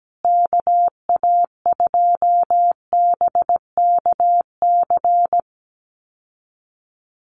Morse code, heard as a keyed tone, about 17 wpm, 700 Hz. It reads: KA2BKC
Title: USA Call Signs Repeated 3x 17wpm